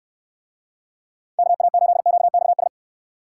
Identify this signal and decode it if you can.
Morse code, heard as a keyed tone, about 67 wpm, 700 Hz. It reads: KA8QXS